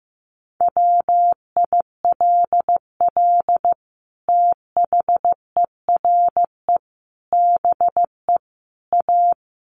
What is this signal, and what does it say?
Morse code, heard as a keyed tone, about 15 wpm, 700 Hz. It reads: WILL THERE BE A
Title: Sentences from Top 1400 Words 15wpm